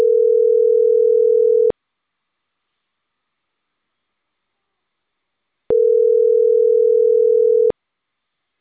Ringback tone (North America), 440 + 480 Hz.